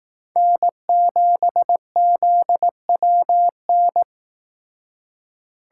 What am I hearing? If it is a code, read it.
Morse code, heard as a keyed tone, about 18 wpm, 700 Hz. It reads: N7ZWN
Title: Speed Racing - USA 1x3 Call Signs 15wpm